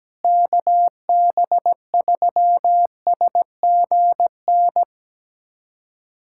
Morse code, heard as a keyed tone, about 17 wpm, 700 Hz. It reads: KB3SGN
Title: USA Callsign DE Callsign 17wpm